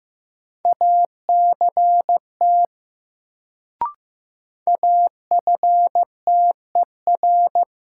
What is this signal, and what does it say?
Morse code, heard as a keyed tone, about 15 wpm, 700 Hz. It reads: ACT AFTER
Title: Sentences from Top 200 Words - New Word Drill - 15wpm